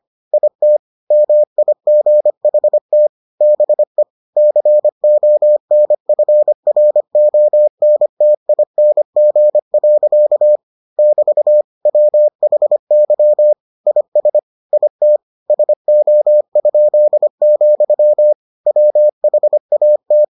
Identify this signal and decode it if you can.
Morse code, heard as a keyed tone, about 25 wpm, 600 Hz. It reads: IT MIGHT BE CONFRONTING. <BT> WHY IS IT SO?, WHAT